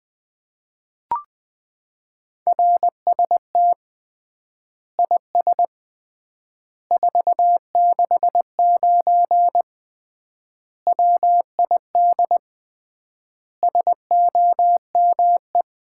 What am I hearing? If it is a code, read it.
Morse code, heard as a keyed tone, about 20 wpm, 700 Hz. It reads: RST IS 469 WID SOME
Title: Common QSO Phrases - 3x Word-Spacing - 20wpm